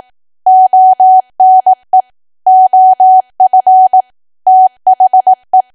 Morse code, heard as a keyed tone, about 18 wpm, 750 Hz. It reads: ONE OF THE
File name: Archive / 18 WPM / 18WPM